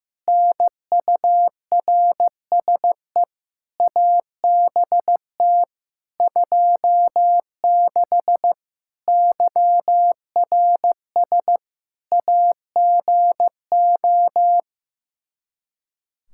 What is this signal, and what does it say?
Morse code, heard as a keyed tone, about 16 wpm, 700 Hz. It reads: NURSE ABT 26 YRS AGO
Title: Practice QSO - Exchange 3 - More Info - 15wpm